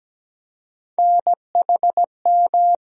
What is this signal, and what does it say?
Morse code, heard as a keyed tone, about 17 wpm, 700 Hz. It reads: NHM
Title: USA Call Sign Postfixes - 3 Letters 17wpm